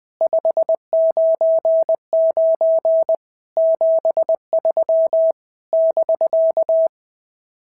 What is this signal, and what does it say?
Morse code, heard as a keyed tone, about 20 wpm, 650 Hz. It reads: 599 73 <BK>
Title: SOTA QSO Practice 20wpm